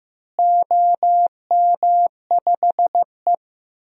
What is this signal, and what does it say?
Morse code, heard as a keyed tone, about 15 wpm, 700 Hz. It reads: OM5E
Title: DXCC-100 Call Signs - 4 Characters - Repeated 2x - 15wpm